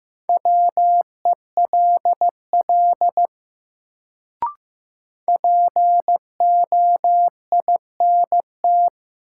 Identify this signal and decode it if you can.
Morse code, heard as a keyed tone, about 16 wpm, 700 Hz. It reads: WELL POINT
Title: Sentences from Top 200 Words - New Word Drill - 15wpm